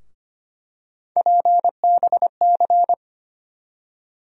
Morse code, heard as a keyed tone, about 25 wpm, 700 Hz. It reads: PBC
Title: USA Call Sign Postfixes - 3 Letters 25wpm